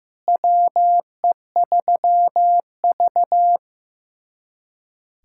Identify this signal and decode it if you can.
Morse code, heard as a keyed tone, about 15 wpm, 700 Hz. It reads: WE3V
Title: USA Callsign DE Callsign - 4 Characters - 15wpm